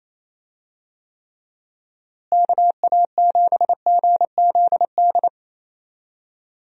Morse code, heard as a keyed tone, about 27 wpm, 700 Hz. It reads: KA7GZD
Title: USA Call Signs - 6 Characters - Repeated 2x - 28wpm